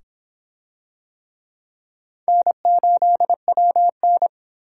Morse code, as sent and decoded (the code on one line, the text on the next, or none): -. ---.. .-- -.
N8WN